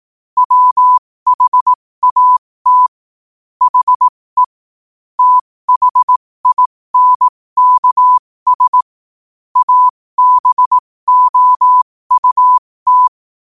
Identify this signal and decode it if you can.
Morse code, heard as a keyed tone, about 18 wpm, spaced as for 15 wpm, 1000 Hz. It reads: WHAT HE THINKS ABOUT
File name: qotdfiles / 15wpm